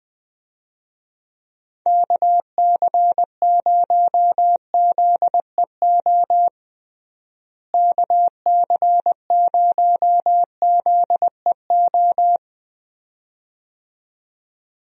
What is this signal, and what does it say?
Morse code, heard as a keyed tone, about 20 wpm, 700 Hz. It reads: KC0ZEO KC0ZEO
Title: USA Call Signs - 6 Characters - Repeated 2x - 20wpm